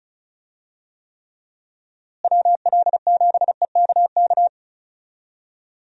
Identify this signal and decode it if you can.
Morse code, heard as a keyed tone, about 34 wpm, 700 Hz. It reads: WL7EKK
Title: USA Call Signs - 6 Characters - Repeated 3x - 35wpm